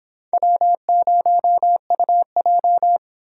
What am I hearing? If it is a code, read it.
Morse code, heard as a keyed tone, about 26 wpm, 700 Hz. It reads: W0UJ